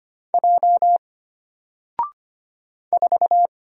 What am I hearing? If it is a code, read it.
Morse code, heard as a keyed tone, about 25 wpm, 700 Hz. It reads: J 4
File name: Single-Letter-Number-25wpm